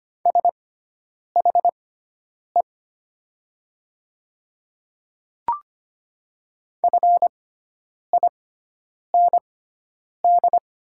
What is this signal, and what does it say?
Morse code, heard as a keyed tone, about 25 wpm, spaced as for 9 wpm, 700 Hz. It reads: SHE FIND
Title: 6x Character Spacing - Top 100 Words - 25wpm